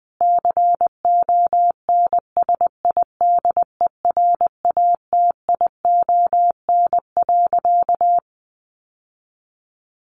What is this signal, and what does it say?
Morse code, heard as a keyed tone, about 20 wpm, 700 Hz. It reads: CONSIDERATION.
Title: Sentences from Top 1500 Words 20wpm